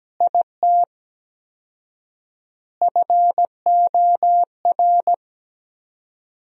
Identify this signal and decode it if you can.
Morse code, heard as a keyed tone, about 17 wpm, 700 Hz. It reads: IT FOR
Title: 4x Word-Spacing - Sentences from Top 200 Words - 17wpm